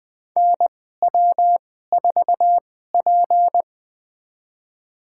Morse code, heard as a keyed tone, about 20 wpm, spaced as for 14 wpm, 700 Hz. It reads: NW4P
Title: USA Call Signs - 4 Characters - 2x Character Spacing - 20wpm